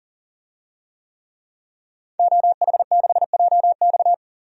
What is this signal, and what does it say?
Morse code, heard as a keyed tone, about 40 wpm, 700 Hz. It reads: OH6JX
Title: DXCC-100 Call Signs - 5 Characters - 40wpm